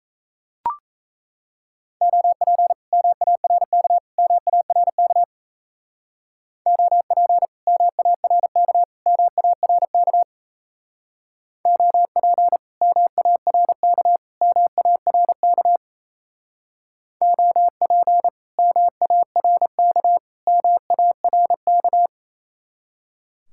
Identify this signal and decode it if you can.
Morse code, heard as a keyed tone, about 34 wpm, 700 Hz. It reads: OP MARK MARK OP MARK MARK OP MARK MARK OP MARK MARK
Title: Speed-Racing Common QSO Phrases - 28wpm